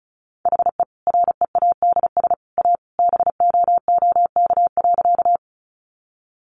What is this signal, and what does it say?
Morse code, heard as a keyed tone, about 34 wpm, 700 Hz. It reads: HE READS A BOOK.
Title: Additional Easy Phrases 35wpm